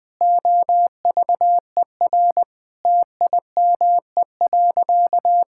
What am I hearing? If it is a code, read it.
Morse code, heard as a keyed tone, about 21 wpm, 700 Hz. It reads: OVER TIME.
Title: Sentences from Top 1000 Words - Review Part 5 - 20wpm